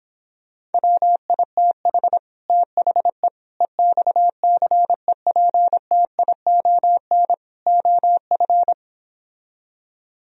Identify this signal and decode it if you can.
Morse code, heard as a keyed tone, about 29 wpm, 700 Hz. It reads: WITH THE EXCEPTION OF